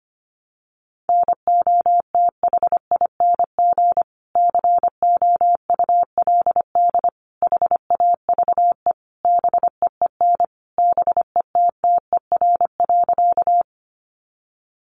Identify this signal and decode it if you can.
Morse code, heard as a keyed tone, about 25 wpm, 700 Hz. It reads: NOTHING COULD HAVE BEEN BETTER.
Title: Sentences from Top 500 Words 25wpm